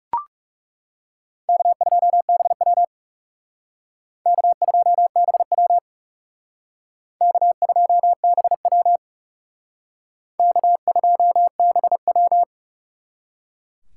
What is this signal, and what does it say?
Morse code, heard as a keyed tone, about 37 wpm, 700 Hz. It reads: K2BW K2BW K2BW K2BW
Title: Speed Racing - USA Call Signs 30wpm